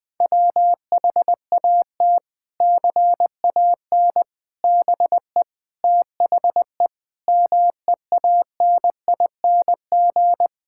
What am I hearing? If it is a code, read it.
Morse code, heard as a keyed tone, about 21 wpm, 700 Hz. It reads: WHAT CAN BE THE MEANING
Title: Sentences from Top 1000 Words - Review Part 6 - 20wpm